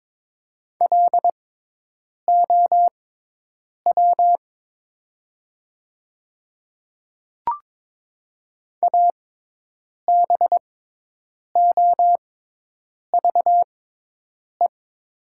Morse code, heard as a keyed tone, about 22 wpm, spaced as for 8 wpm, 700 Hz. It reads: LOW ABOVE
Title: 6x Character Spacing - Top 500 Words - 22wpm